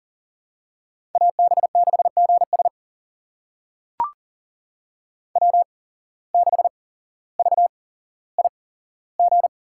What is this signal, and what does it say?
Morse code, heard as a keyed tone, about 40 wpm, spaced as for 17 wpm, 700 Hz. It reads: AB6GS W6VIG
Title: USA Call Signs - 5 Characters - 8x Character Spacing - 40wpm